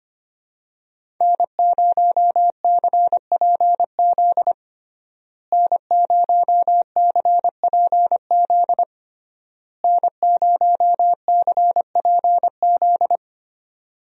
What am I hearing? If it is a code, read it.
Morse code, heard as a keyed tone, about 25 wpm, 700 Hz. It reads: N0CPZ N0CPZ N0CPZ
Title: USA Call Signs - 5 Characters - Repeated 3x - 25wpm